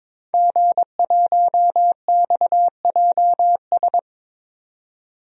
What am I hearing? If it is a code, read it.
Morse code, heard as a keyed tone, about 22 wpm, 700 Hz. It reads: G1XJS
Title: DXCC-100 Call Signs - 5 Characters - 22wpm